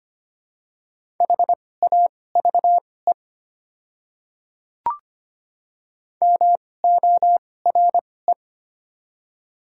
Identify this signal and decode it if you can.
Morse code, heard as a keyed tone, about 25 wpm, spaced as for 18 wpm, 690 Hz. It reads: HAVE MORE
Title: 2x Character Spacing - Top 300 Words - 25wpm